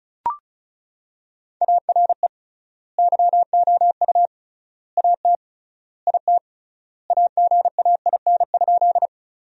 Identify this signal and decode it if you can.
Morse code, heard as a keyed tone, about 36 wpm, 700 Hz. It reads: ARE YOU AT IT AGAIN?
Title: 3x Word-Spacing - Sentences from Top 200 Words - 35wpm